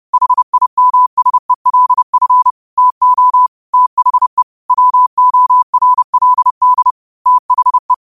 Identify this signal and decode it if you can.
Morse code, heard as a keyed tone, about 30 wpm, 1000 Hz. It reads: HIMSELF TO THE WORLD THE